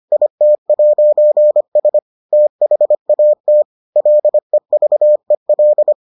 Morse code, heard as a keyed tone, about 25 wpm, 600 Hz. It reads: IT'S THAT LEVEL